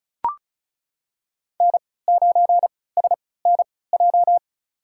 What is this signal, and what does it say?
Morse code, heard as a keyed tone, about 34 wpm, spaced as for 20 wpm, 700 Hz. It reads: N9SNJ
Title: USA Call Signs - 5 Characters - 3x Character Spacing - 35wpm